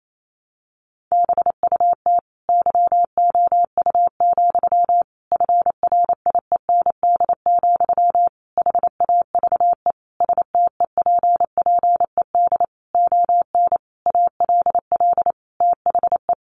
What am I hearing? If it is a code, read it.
Morse code, heard as a keyed tone, about 28 wpm, 700 Hz. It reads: BUT YOU, FRIEND, HAVE STEPPED ON ALL THE
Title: A Sound Of Thunder 28wpm